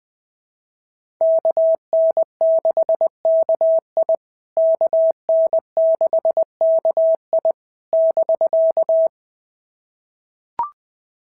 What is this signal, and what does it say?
Morse code, heard as a keyed tone, about 20 wpm, 650 Hz. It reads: KN6KI KN6KI <BK>
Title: POTA QSO Practice - 20wpm